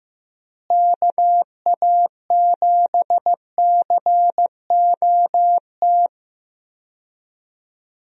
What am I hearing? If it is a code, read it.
Morse code, heard as a keyed tone, about 15 wpm, 700 Hz. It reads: KA7COT
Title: USA Call Signs - 6 Characters - Repeated 2x - 15wpm